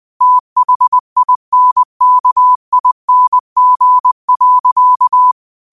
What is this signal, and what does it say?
Morse code, heard as a keyed tone, about 20 wpm, 1000 Hz. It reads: THINKING.
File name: qotdfiles / 20wpm